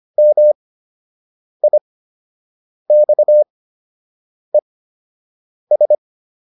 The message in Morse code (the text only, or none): MIXES